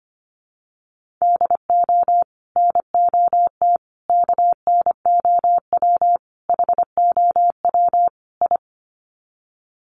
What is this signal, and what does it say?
Morse code, heard as a keyed tone, about 25 wpm, 700 Hz. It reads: DO NOT KNOW HOW I
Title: Sets of 5 Words Encore2 25wpm